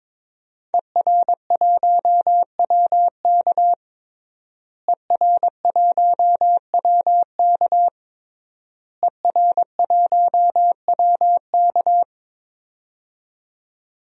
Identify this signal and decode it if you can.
Morse code, heard as a keyed tone, about 22 wpm, 700 Hz. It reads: ER1WK ER1WK ER1WK
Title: DXCC-100 Call Signs - 5 Characters - Repeated 3x - 22wpm